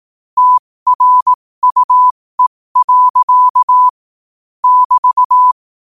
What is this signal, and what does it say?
Morse code, heard as a keyed tone, about 18 wpm, spaced as for 15 wpm, 1000 Hz. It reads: TRUE. <BT>
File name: qotdfiles / 15wpm